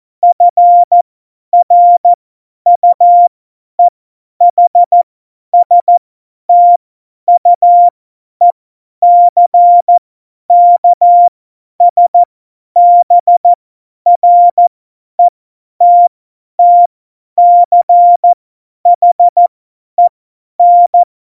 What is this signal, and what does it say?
Morse code, heard as a keyed tone, about 14 wpm, spaced as for 10 wpm, 700 Hz. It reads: FRUEHSTUECKSBRETTCHEN
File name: Morsekurs / Witze / Witz 26. Juni 2018, 50 F70